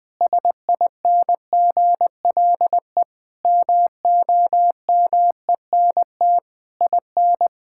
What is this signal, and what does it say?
Morse code, heard as a keyed tone, about 20 wpm, 700 Hz. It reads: SINGLE MOMENT IN
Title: Sentences from Top 1000 Words - Review Part 6 - 20wpm